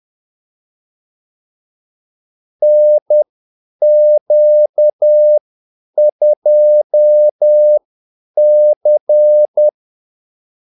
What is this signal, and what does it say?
Morse code, heard as a keyed tone, about 10 wpm, spaced as for 8 wpm, 600 Hz. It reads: NQ2C
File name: Callsigns-10-8wpm